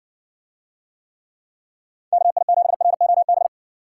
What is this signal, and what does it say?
Morse code, heard as a keyed tone, about 59 wpm, 700 Hz. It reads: KI7ROB